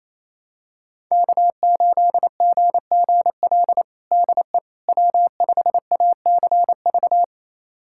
Morse code, heard as a keyed tone, about 27 wpm, 700 Hz. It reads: K8GGL DE W5ACV
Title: USA Callsign DE Callsign - 5 Characters - 28wpm